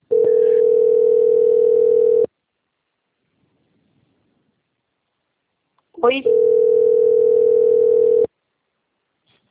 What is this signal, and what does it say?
Ringback tone (North America), 440 + 480 Hz, with speech.
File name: out- - - - - .11614